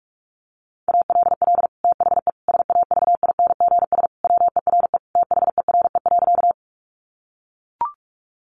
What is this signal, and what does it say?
Morse code, heard as a keyed tone, about 44 wpm, 700 Hz. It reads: ALL THE SAVINGS WERE THERE.
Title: Sentences from Top 1200 Words 45wpm